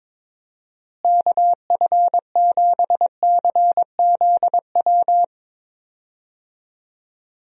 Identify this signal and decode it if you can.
Morse code, heard as a keyed tone, about 21 wpm, 700 Hz. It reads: KF7CZW
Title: USA Call Signs - 6 Characters - Repeated 3x - 22wpm